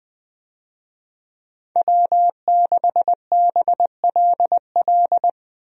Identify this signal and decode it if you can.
Morse code, heard as a keyed tone, about 21 wpm, 700 Hz. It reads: W6BLL
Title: USA Call Signs - 5 Characters - Repeated 2x - 20wpm